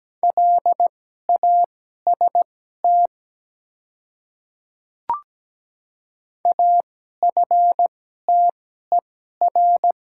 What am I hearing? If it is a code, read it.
Morse code, heard as a keyed tone, about 17 wpm, spaced as for 12 wpm, 700 Hz. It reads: LAST AFTER
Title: 2x Character Spacing - Top 300 Words - 17wpm